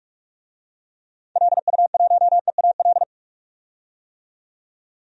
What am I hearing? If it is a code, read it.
Morse code, heard as a keyed tone, about 41 wpm, 700 Hz. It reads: LU1EAL
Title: DXCC-100 Call Signs - 6 Characters - 45wpm